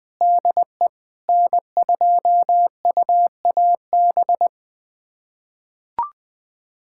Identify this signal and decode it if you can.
Morse code, heard as a keyed tone, about 21 wpm, 700 Hz. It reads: DE N2UAB
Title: USA Callsign DE Callsign - 5 Characters - 20wpm